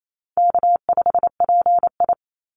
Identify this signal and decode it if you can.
Morse code, heard as a keyed tone, about 28 wpm, 700 Hz. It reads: K5PI
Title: Easy Call Signs Repeated 3x - 28wpm